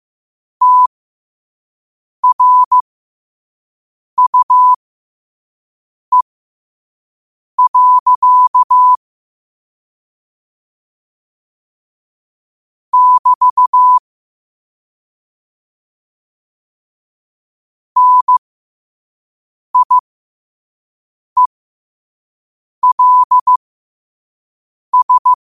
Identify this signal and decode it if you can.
Morse code, heard as a keyed tone, about 15 wpm, spaced as for 5 wpm, 1000 Hz. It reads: TRUE. <BT> NIELS
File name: qotdfiles / 5wpm